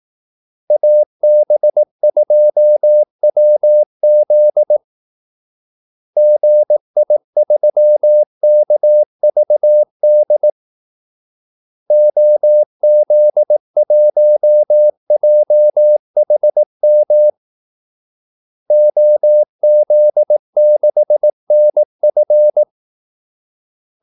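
Morse code, heard as a keyed tone, about 18 wpm, 600 Hz. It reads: AB2WZ GI3KVD OZ1JHM OZ6NF
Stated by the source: Morse call-sign practice at 18 wpm.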